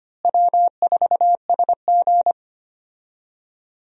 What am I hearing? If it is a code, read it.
Morse code, heard as a keyed tone, about 25 wpm, 700 Hz. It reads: W4SG
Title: USA Call Signs - 4 Characters - Repeated 2x - 25wpm